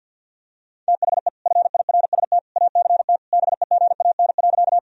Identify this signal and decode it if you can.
Morse code, heard as a keyed tone, about 53 wpm, 700 Hz. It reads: THE FIRST ACT BEGAN.
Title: Sentences from Top 1000 Words - Review Part 1 - 50wpm